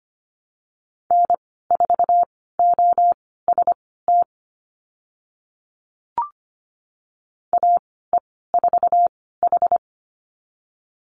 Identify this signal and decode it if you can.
Morse code, heard as a keyed tone, about 25 wpm, spaced as for 16 wpm, 700 Hz. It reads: N4OST AE4H
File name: Lesson 15g - Number 5 - Call Signs 25wpm